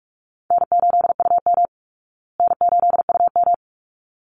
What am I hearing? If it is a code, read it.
Morse code, heard as a keyed tone, about 44 wpm, 700 Hz. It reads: N8UM N8UM
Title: Easy Call Signs Repeated 2x - 45wpm